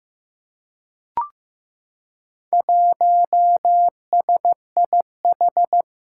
Morse code, heard as a keyed tone, about 15 wpm, 700 Hz. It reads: 1SIH
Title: Four Challenging Letter-Number Combinations 15wpm